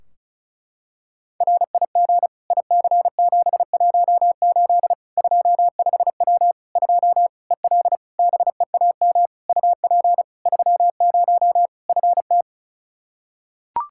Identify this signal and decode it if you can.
Morse code, heard as a keyed tone, about 34 wpm, 700 Hz. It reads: RIG IC718 25W 2 EL BEAM UP 30 FT
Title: Practice QSO - Exchange 2 - Info - 35wpm